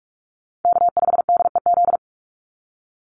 Morse code, heard as a keyed tone, about 44 wpm, 690 Hz. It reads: K5DEZ
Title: NA SOTA Chaser Callsigns 45wpm